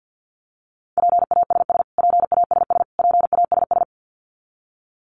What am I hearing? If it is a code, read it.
Morse code, heard as a keyed tone, about 50 wpm, 700 Hz. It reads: PASS PASS PASS
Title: Top 500 Words Repeated 50wpm